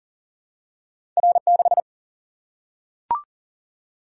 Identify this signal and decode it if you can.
Morse code, heard as a keyed tone, about 42 wpm, 700 Hz. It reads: R 6
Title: Single-Letter Single-Number 40wpm